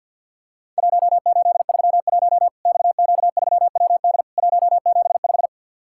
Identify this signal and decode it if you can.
Morse code, heard as a keyed tone, about 50 wpm, 700 Hz. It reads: 1831 XQ3WD 175
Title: Worked All Europe DX Contest - QTC Exchange - 50wpm